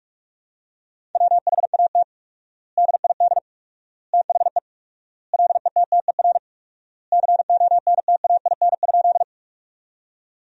Morse code, heard as a keyed tone, about 46 wpm, 700 Hz. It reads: WHAT DID THE LETTER CONTAIN?
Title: 4x Word-Spacing - Sentences from Top 500 Words - 45wpm